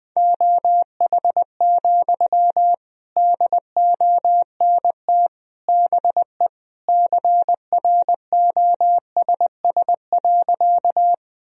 Morse code, heard as a keyed tone, about 20 wpm, 700 Hz. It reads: OH, DONT BE CROSS.
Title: Sentences from Top 1000 Words - Review Part 1 - 20wpm